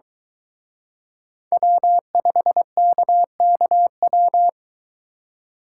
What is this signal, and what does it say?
Morse code, heard as a keyed tone, about 23 wpm, 700 Hz. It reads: W5KKW